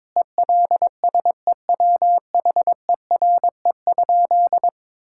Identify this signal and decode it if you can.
Morse code, heard as a keyed tone, about 22 wpm, 700 Hz. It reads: ELSEWHERE?
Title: Sentences from Top 1600 Words 22wpm